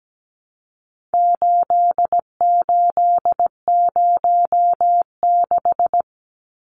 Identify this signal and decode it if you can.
Morse code, heard as a keyed tone, about 17 wpm, 700 Hz. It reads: 8806
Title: Numbers - 4-Digit 17wpm